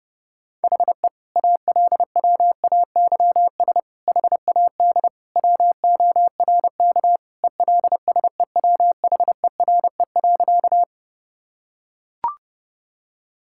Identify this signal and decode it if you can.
Morse code, heard as a keyed tone, about 32 wpm, 700 Hz. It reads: HE ALWAYS HAD WORK ELSEWHERE.
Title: Sentences from Top 1600 Words 30wpm